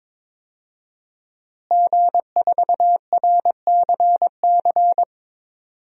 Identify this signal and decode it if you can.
Morse code, heard as a keyed tone, about 22 wpm, 700 Hz. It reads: G4RCC
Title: DXCC-100 Call Signs - 5 Characters - Repeated 2x - 22wpm